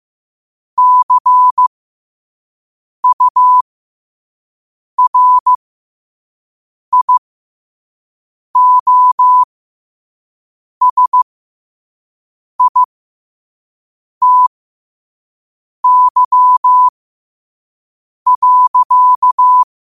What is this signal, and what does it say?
Morse code, heard as a keyed tone, about 15 wpm, spaced as for 5 wpm, 1000 Hz. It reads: CURIOSITY.
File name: qotdfiles / 5wpm